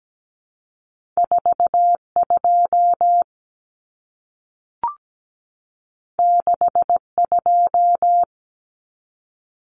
Morse code, heard as a keyed tone, about 17 wpm, 700 Hz. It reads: 42 62
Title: Common Cut Numbers - 2-Digit 17wpm